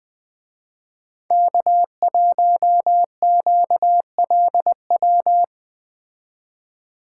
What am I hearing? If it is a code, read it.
Morse code, heard as a keyed tone, about 20 wpm, 700 Hz. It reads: K1QLW